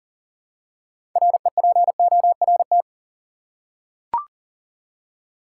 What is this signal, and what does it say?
Morse code, heard as a keyed tone, about 40 wpm, 700 Hz. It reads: REPORT
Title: Six Letter Words - 40wpm